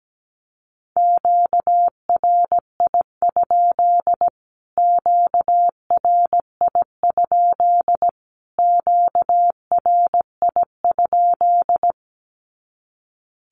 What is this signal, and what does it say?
Morse code, heard as a keyed tone, about 17 wpm, 700 Hz. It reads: QRI? QRI? QRI?
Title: Q-Codes Repeated 3x 17wpm